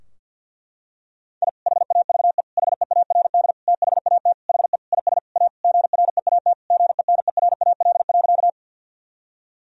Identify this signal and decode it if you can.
Morse code, heard as a keyed tone, about 52 wpm, 700 Hz. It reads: I HAVE HEARD THAT HE IS A GREAT GENERAL.
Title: Sentences from Top 1000 Words - Review Part 3 - 50wpm